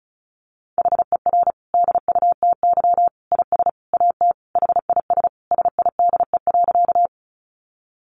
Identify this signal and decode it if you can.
Morse code, heard as a keyed tone, about 34 wpm, 700 Hz. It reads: HER DUTY IS AT HIS SIDE.
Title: Sentences from Top 1500 Words 35wpm